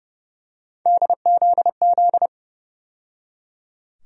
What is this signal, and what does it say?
Morse code, heard as a keyed tone, about 30 wpm, 700 Hz. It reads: DZZ
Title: USA Call Sign Postfixes - 3 Letters 30wpm